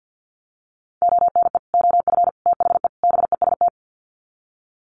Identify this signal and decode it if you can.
Morse code, heard as a keyed tone, about 48 wpm, 700 Hz. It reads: ONE OF THE BEST
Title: Sets of 4 Words 50wpm